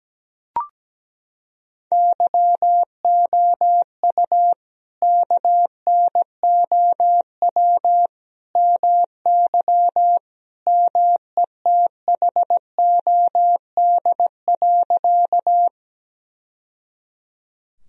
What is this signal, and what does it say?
Morse code, heard as a keyed tone, about 17 wpm, 700 Hz. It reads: YOU KNOW MY METHOD.
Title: Sentences from Top 1000 Words - Review Part 3 - 17wpm